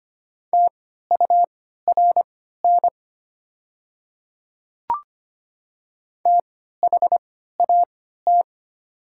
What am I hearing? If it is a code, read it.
Morse code, heard as a keyed tone, about 25 wpm, spaced as for 14 wpm, 690 Hz. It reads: TURN THAT
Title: 3x Character Spacing - Top 500 Words - 25wpm